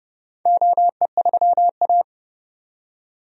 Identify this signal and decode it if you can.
Morse code, heard as a keyed tone, about 30 wpm, 700 Hz. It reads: OE3A
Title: DXCC-100 Call Signs - 4 Characters - 30wpm